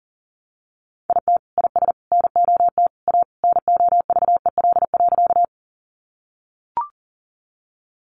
Morse code, heard as a keyed tone, about 37 wpm, 700 Hz. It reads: IT IS NOT A NOVEL.
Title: Sentences from Top 1300 Words 40wpm